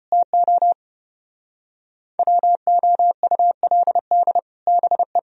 Morse code, heard as a keyed tone, about 27 wpm, 700 Hz. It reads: TO WOULD BE
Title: Speed-Racing - Sets of 4 Words 25wpm